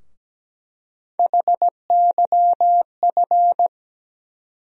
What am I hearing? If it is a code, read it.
Morse code, heard as a keyed tone, about 17 wpm, 700 Hz. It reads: HYF
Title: USA Call Sign Postfixes - 3 Letters 17wpm